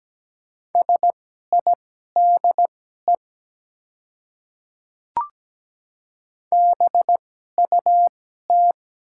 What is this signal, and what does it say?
Morse code, heard as a keyed tone, about 17 wpm, spaced as for 12 wpm, 700 Hz. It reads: SIDE BUT
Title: 2x Character Spacing - Top 100 Words - 17wpm